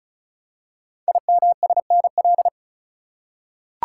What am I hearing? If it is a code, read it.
Morse code, heard as a keyed tone, about 36 wpm, 700 Hz. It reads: IMSNL
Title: Code Groups - 5 Letters - 35wpm